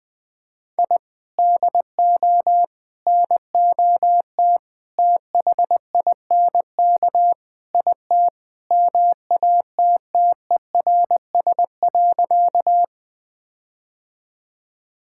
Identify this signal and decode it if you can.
Morse code, heard as a keyed tone, about 20 wpm, 700 Hz. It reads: I DO NOT THINK IT MATTERS.
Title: Sentences with 5-Word Phrases - Spoken Beginning - 20wpm